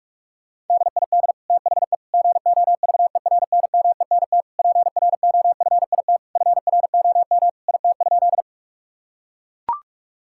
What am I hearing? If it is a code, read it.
Morse code, heard as a keyed tone, about 41 wpm, 700 Hz. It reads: DID THE GOVERNMENT PROFIT FROM IT?
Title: Sentences from Top 1000 Words - Review Part 5 - 45wpm